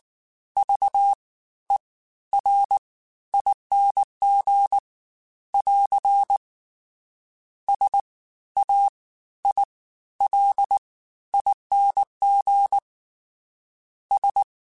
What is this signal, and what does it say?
Morse code, heard as a keyed tone, about 19 wpm, spaced as for 13 wpm, 800 Hz. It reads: VERING <AR> SAILING S
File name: Special INGs-mp3s / Special INGs-mp3s / Fast ing @18